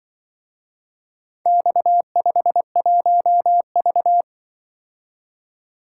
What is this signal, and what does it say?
Morse code, heard as a keyed tone, about 24 wpm, 700 Hz. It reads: X51V
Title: Speed-Racing - Four Letter-Number Combinations - 20wpm